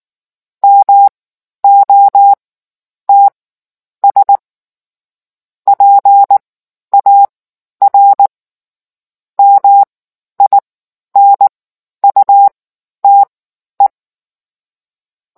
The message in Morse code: MOTS PAR MINUTE

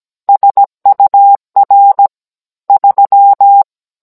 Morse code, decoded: SUR 3

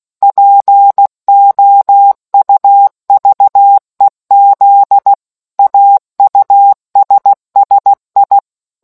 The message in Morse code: POUVEZ AUSSI

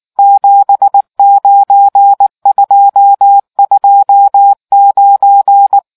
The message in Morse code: 79229

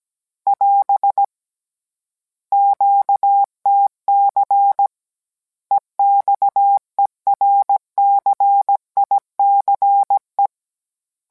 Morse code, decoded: <AS> QTC EXERCICE

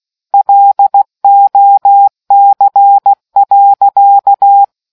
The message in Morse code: LOC.

